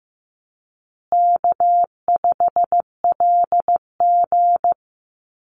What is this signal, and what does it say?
Morse code, heard as a keyed tone, about 15 wpm, 700 Hz. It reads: K5LG
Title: Easy Call Signs Repeated 3x - 15wpm